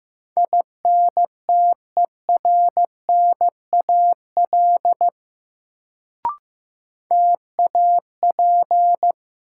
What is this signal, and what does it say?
Morse code, heard as a keyed tone, about 15 wpm, 700 Hz. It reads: INTERNAL TAP
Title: Sentences from Top 1400 Words - New Word Drill Alt - 15wpm